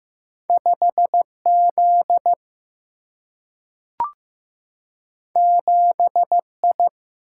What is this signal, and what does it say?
Morse code, heard as a keyed tone, about 15 wpm, 700 Hz. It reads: 5Z 7I
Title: Single-Letter Single-Number 15wpm